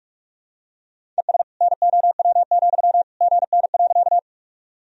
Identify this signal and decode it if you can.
Morse code, heard as a keyed tone, about 43 wpm, 700 Hz. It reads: ES NOW, GN.
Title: Sentences from Top 300 Words and Abbreviations 45wpm